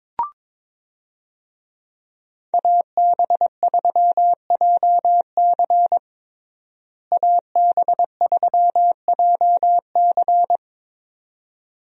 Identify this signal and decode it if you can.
Morse code, heard as a keyed tone, about 22 wpm, 700 Hz. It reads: AB3JC AB3JC
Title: USA Call Signs - 5 Characters - Repeated 2x - 22wpm